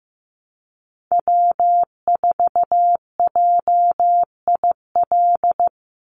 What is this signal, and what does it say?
Morse code, heard as a keyed tone, about 15 wpm, 700 Hz. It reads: W4JIL
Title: USA Call Signs 15wpm